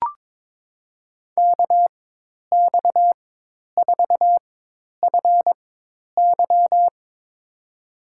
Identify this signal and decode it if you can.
Morse code, heard as a keyed tone, about 21 wpm, spaced as for 10 wpm, 700 Hz. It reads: KX4FY
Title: USA Call Signs - 5 Characters - 4x Character Spacing - 22wpm